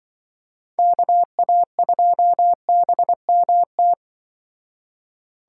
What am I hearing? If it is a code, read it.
Morse code, heard as a keyed tone, about 25 wpm, 700 Hz. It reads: KA2BMT